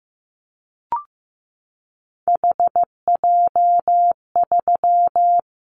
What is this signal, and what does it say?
Morse code, heard as a keyed tone, about 15 wpm, 700 Hz. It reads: HJ3
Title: Callsign Prefixes for Top 100 DXCC - 15wpm